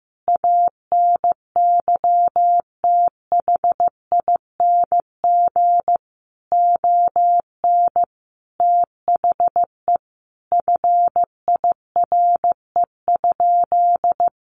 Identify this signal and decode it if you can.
Morse code, heard as a keyed tone, about 15 wpm, 700 Hz. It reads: ANYTHING ON THE FIRE?
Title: Sentences from Top 1000 Words 15wpm